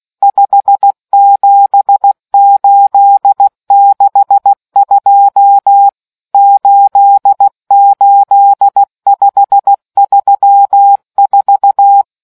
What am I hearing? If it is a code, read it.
Morse code, heard as a keyed tone, about 16 wpm, 800 Hz. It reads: 57862 88534